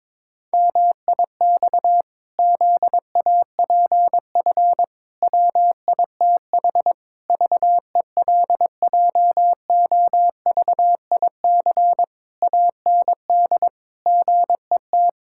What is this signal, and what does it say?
Morse code, heard as a keyed tone, about 22 wpm, 700 Hz. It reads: MIX ZAPF WITH VELJOVIC AND GET
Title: Pangram Challenge v2 - 22wpm